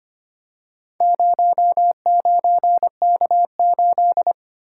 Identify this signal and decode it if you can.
Morse code, heard as a keyed tone, about 25 wpm, 700 Hz. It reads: 09K8
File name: practice / 25wpm / Four Letter-Number Combinations - 25wpm